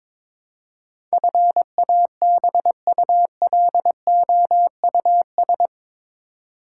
Morse code, heard as a keyed tone, about 22 wpm, 700 Hz. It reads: FABULOUS
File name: practice / 22wpm / Eight Letter Words - 22wpm